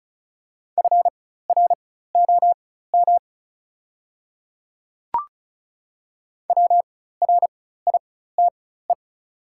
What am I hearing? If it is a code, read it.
Morse code, heard as a keyed tone, about 36 wpm, spaced as for 16 wpm, 690 Hz. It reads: FROM WRITE
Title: 4x Character Spacing - Top 200 Words - 35wpm